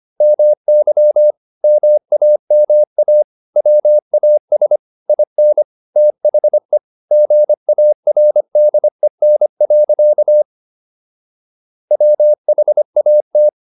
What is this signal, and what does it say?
Morse code, heard as a keyed tone, about 25 wpm, 600 Hz. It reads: MY MAMA WAS IN THE GARDEN. WHAT